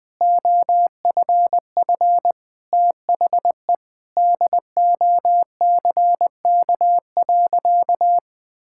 Morse code, heard as a keyed tone, about 21 wpm, 700 Hz. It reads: OFF THE DOCK.
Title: Sentences with 2-Word Phrases - Spoken Beginning - 20wpm